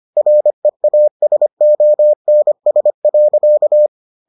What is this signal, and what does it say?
Morse code, heard as a keyed tone, about 25 wpm, 600 Hz. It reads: REASONS.